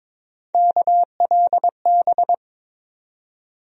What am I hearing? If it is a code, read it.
Morse code, heard as a keyed tone, about 21 wpm, 700 Hz. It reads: KLB
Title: USA Call Sign Postfixes - 3 Letters 22wpm